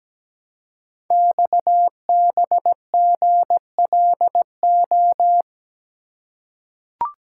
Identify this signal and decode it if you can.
Morse code, heard as a keyed tone, about 17 wpm, 700 Hz. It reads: XBGLO
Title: Code Groups - 5 Letter:Numbers - 17wpm